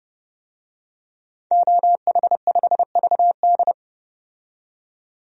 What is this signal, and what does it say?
Morse code, heard as a keyed tone, about 30 wpm, 700 Hz. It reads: OH5VD
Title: DXCC-100 Call Signs - 5 Characters - 30wpm